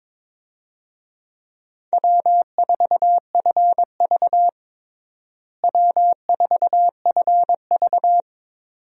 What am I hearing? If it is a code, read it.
Morse code, heard as a keyed tone, about 22 wpm, 700 Hz. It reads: W4FV W4FV
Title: USA Call Signs - 4 Characters - Repeated 2x - 22wpm